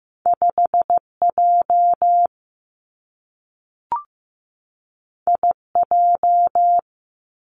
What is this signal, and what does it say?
Morse code, heard as a keyed tone, about 15 wpm, 700 Hz. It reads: 5J IJ
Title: Two Challenging Letter-Number Combinations 15wpm